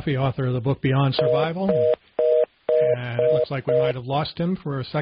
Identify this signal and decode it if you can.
Reorder tone (North America), 480 + 620 Hz, with speech.